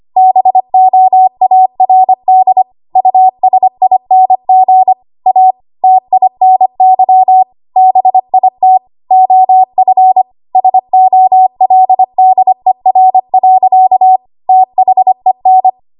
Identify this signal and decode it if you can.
Morse code, heard as a keyed tone, about 25 wpm, 750 Hz. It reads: BOARD USING A TINY BIT OF SOLDER. THEN
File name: Archive / 25 WPM / 25WPM